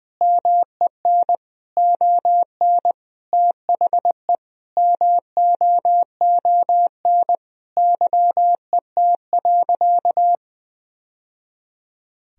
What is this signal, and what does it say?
Morse code, heard as a keyed tone, about 20 wpm, 700 Hz. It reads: MEN ON THE MOON YET.
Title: Sentences from Top 1000 Words - Review Part 3 - 20wpm